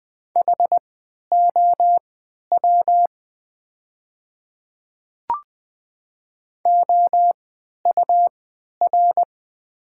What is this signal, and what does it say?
Morse code, heard as a keyed tone, about 20 wpm, spaced as for 11 wpm, 700 Hz. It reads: HOW OUR
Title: 3x Character Spacing - Top 200 Words - 20wpm